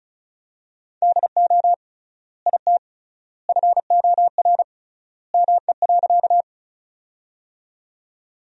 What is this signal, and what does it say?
Morse code, heard as a keyed tone, about 36 wpm, 700 Hz. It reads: DO IT FOR ME.
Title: 3x Word-Spacing - Sentences from Top 200 Words - 35wpm